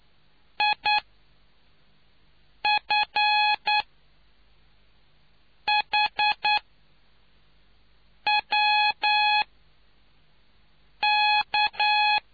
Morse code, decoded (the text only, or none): IFHWK